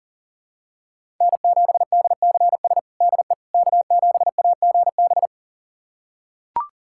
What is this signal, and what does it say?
Morse code, heard as a keyed tone, about 40 wpm, 700 Hz. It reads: N7DCS DE K7AGB
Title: USA Callsign DE Callsign 40wpm